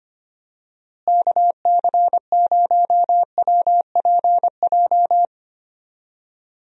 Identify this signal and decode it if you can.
Morse code, heard as a keyed tone, about 25 wpm, 700 Hz. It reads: KC0WPJ
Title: USA Call Signs - 6 Characters - Repeated 2x - 25wpm